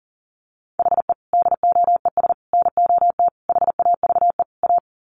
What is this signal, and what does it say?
Morse code, heard as a keyed tone, about 40 wpm, 700 Hz. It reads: HE DOES NOT HAVE A
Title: Sets of 5 Words Encore 40wpm